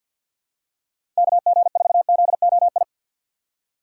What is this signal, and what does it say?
Morse code, heard as a keyed tone, about 52 wpm, 700 Hz. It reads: KG4ZOI
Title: USA Call Signs - 6 Characters - 8x Character Spacing - 50wpm